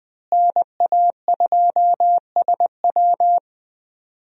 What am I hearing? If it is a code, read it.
Morse code, heard as a keyed tone, about 20 wpm, 700 Hz. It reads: NA2SW
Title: USA Call Signs - 5 Characters - Repeated 3x - 20wpm